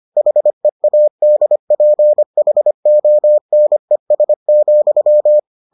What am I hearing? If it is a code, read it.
Morse code, heard as a keyed tone, about 25 wpm, 600 Hz. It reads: HEADPHONES,